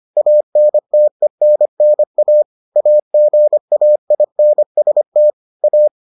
Morse code, heard as a keyed tone, about 25 wpm, 600 Hz. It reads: ANTENNA AGAINST A